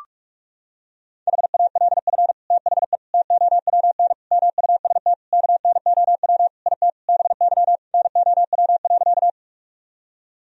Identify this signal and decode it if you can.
Morse code, heard as a keyed tone, about 46 wpm, 700 Hz. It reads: HALF THE TOWN MUST KNOW IT BY NOW.
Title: Sentences from Top 1000 Words - Review Part 2 - 45wpm